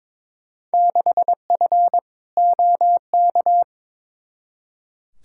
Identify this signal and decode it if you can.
Morse code, heard as a keyed tone, about 22 wpm, 700 Hz. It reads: 6F OK
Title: Field Day Exchanges Encore - 22wpm